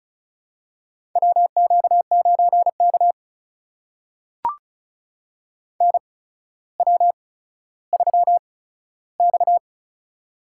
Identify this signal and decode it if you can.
Morse code, heard as a keyed tone, about 36 wpm, spaced as for 15 wpm, 700 Hz. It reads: WQ9K NW3X
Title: USA Call Signs - 4 Characters - 8x Character Spacing - 35wpm